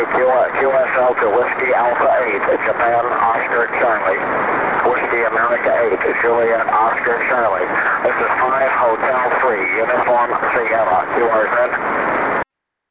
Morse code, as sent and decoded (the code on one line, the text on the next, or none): none